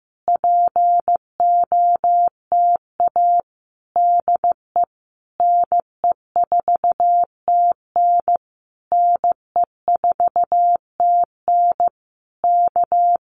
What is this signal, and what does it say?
Morse code, heard as a keyed tone, about 15 wpm, 700 Hz. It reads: POTA DE NE4TN NE4TN K
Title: Common QSO Phrases Encore 15wpm